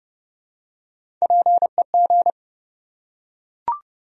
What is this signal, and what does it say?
Morse code, heard as a keyed tone, about 30 wpm, 700 Hz. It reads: PEG
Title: USA Common Names - Max 3 Letters - 30wpm